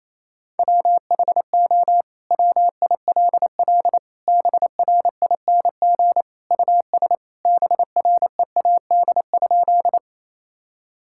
Morse code, heard as a keyed tone, about 28 wpm, 700 Hz. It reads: WHO WILL BRING US BREAD?
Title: Sentences from Top 1600 Words 28wpm